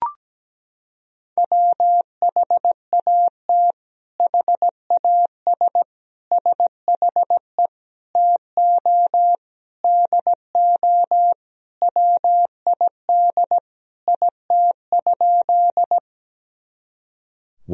Morse code, heard as a keyed tone, about 17 wpm, 700 Hz. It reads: WHAT HAS SHE TO DO WID IT?
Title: Sentences from Top 100 Words and Abbreviations 17wpm